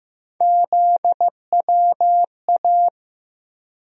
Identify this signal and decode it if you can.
Morse code, heard as a keyed tone, about 15 wpm, 700 Hz. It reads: ZWA
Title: USA Call Sign Postfixes - 3 Letters 15wpm